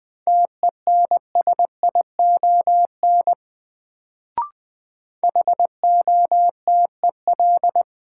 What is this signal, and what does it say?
Morse code, heard as a keyed tone, about 20 wpm, 700 Hz. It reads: TENSION HOTEL
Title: Sentences from Top 1600 Words - New Word Drill Alt - 20wpm